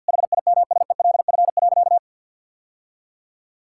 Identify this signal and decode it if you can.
Morse code, heard as a keyed tone, about 51 wpm, 700 Hz. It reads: HIMSELF.
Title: Sentences from Top 1600 Words 50wpm